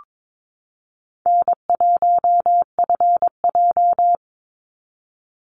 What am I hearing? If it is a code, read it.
Morse code, heard as a keyed tone, about 22 wpm, 700 Hz. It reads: N1FJ
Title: NA SOTA Activator Callsigns 22wpm